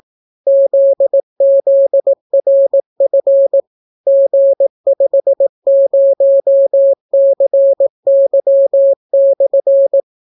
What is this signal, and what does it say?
Morse code, heard as a keyed tone, about 18 wpm, 550 Hz. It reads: ZZRF G50CY/